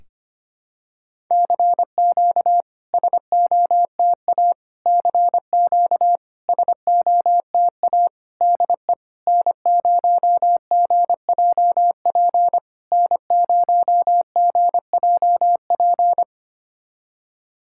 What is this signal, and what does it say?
Morse code, heard as a keyed tone, about 25 wpm, 690 Hz. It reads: CQ SOTA CQ SOTA DE N0GJP N0GJP
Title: SOTA QSO Practice 25wpm